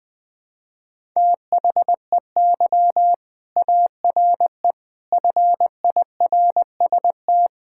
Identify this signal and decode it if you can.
Morse code, heard as a keyed tone, about 20 wpm, 700 Hz. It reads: THEY ARE FIRST
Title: Sentences from Top 1000 Words - Review Part 5 - 20wpm